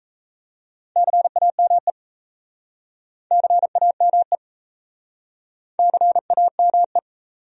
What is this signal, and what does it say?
Morse code, heard as a keyed tone, about 39 wpm, 700 Hz. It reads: CAME CAME CAME